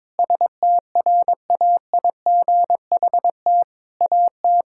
Morse code, heard as a keyed tone, about 22 wpm, 700 Hz. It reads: STRAIGHT AT
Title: Sentences from Top 1000 Words - Review Part 6 - 22wpm